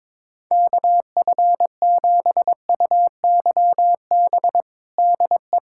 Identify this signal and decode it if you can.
Morse code, heard as a keyed tone, about 22 wpm, 700 Hz. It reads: KF7UYB DE
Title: USA Callsign DE Callsign - 6 Characters - 22wpm